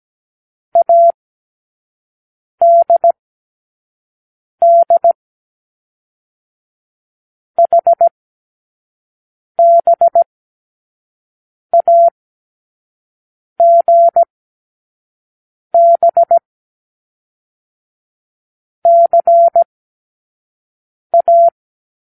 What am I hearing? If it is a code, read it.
Morse code, heard as a keyed tone, about 17 wpm, 680 Hz. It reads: A D D H B A G B C A
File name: k7qo / Lesson12(A-H)